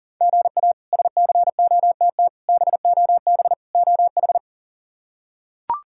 Morse code, heard as a keyed tone, about 40 wpm, 700 Hz. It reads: GA SCOTT BOB OH
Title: K1USN SST QSO Practice - 40wpm